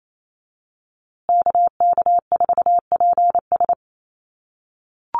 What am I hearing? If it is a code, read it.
Morse code, heard as a keyed tone, about 28 wpm, 700 Hz. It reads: KK4PS
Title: USA Call Signs Repeated 3x 28wpm